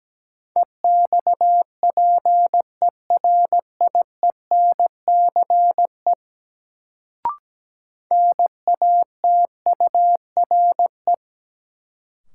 Morse code, heard as a keyed tone, about 17 wpm, 700 Hz. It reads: EXPERIENCE NATURE
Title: Sentences from Top 600 Words - New Word Drill - 17wpm